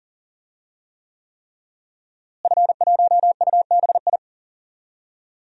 Morse code, heard as a keyed tone, about 40 wpm, 700 Hz. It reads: F1UBI
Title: DXCC-100 Call Signs - 5 Characters - Repeated 3x - 40wpm